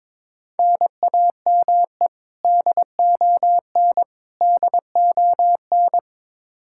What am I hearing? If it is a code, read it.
Morse code, heard as a keyed tone, about 22 wpm, 700 Hz. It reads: NAME DON DON
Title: Practice QSO - Exchange 1 - Beginning - 22wpm